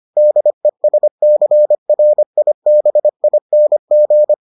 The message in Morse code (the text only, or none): DESCRIBING